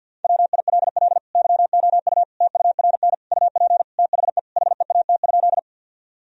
Morse code, heard as a keyed tone, about 52 wpm, 700 Hz. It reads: WILL YOU TURN UP THE HEAT?
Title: Sentences from Top 1000 Words - Review Part 3 - 50wpm